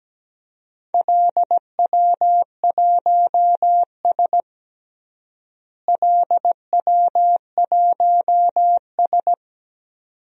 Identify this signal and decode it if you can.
Morse code, heard as a keyed tone, about 17 wpm, 700 Hz. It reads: LW1S LW1S
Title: DXCC-100 Call Signs - 4 Characters - Repeated 2x - 17wpm